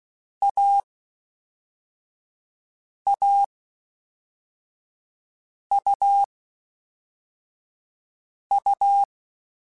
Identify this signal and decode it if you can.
Morse code, heard as a keyed tone, about 16 wpm, 800 Hz. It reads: A A U U
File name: Difficult Combos / Difficult Combos-mp3s / AUV AUV4 @5